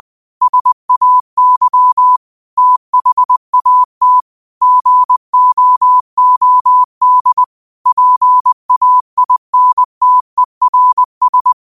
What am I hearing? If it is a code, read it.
Morse code, heard as a keyed tone, about 20 wpm, 1000 Hz. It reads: SAY THAT GOOD PAINTERS